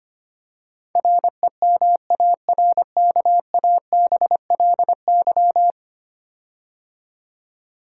Morse code, heard as a keyed tone, about 25 wpm, 700 Hz. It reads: REMARKABLY
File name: 10 Letter Words - First 3 Letters Given - 25wpm